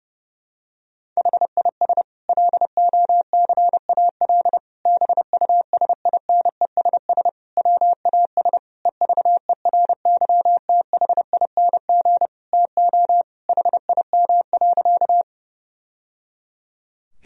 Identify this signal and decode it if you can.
Morse code, heard as a keyed tone, about 30 wpm, 700 Hz. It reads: HIS LOCAL BUSINESS WAS EVERYTHING TO HIM.
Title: Sentences from Top 1000 Words - Review Part 3 - 30wpm